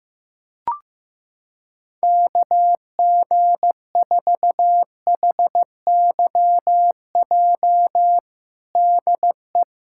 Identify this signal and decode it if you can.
Morse code, heard as a keyed tone, about 15 wpm, 700 Hz. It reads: KG4HYJ DE
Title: USA Callsign DE Callsign - 6 Characters - 15wpm